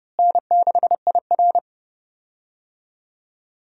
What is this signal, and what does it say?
Morse code, heard as a keyed tone, about 30 wpm, 700 Hz. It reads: N6IR
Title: USA Call Signs - 4 Characters - Repeated 2x - 30wpm